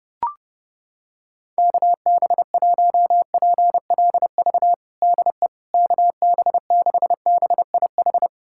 Morse code, heard as a keyed tone, about 31 wpm, 700 Hz. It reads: KB1PLV DE KB6BIH
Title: USA Callsign DE Callsign - 6 Characters - 30wpm